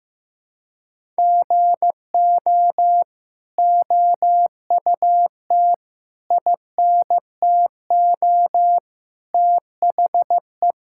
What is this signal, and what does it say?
Morse code, heard as a keyed tone, about 15 wpm, 700 Hz. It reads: GO OUT INTO THE
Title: Sentences from Top 1000 Words - Review Part 5 - 15wpm